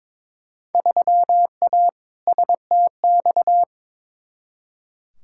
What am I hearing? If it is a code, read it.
Morse code, heard as a keyed tone, about 21 wpm, 700 Hz. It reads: 3A STX
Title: Field Day Exchanges - 22wpm